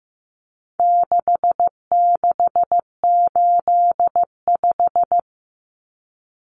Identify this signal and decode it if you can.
Morse code, heard as a keyed tone, about 15 wpm, 700 Hz. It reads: 6685
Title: Common Cut Numbers - 4-Digit 15wpm